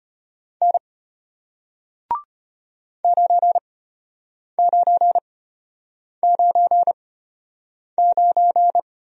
Morse code, heard as a keyed tone, about 33 wpm, spaced as for 8 wpm, 700 Hz. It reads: N 9999